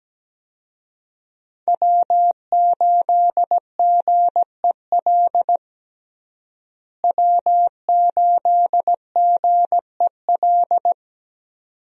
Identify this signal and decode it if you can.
Morse code, heard as a keyed tone, about 17 wpm, 700 Hz. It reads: W8GEL W8GEL
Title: USA Call Signs - 5 Characters - Repeated 2x - 17wpm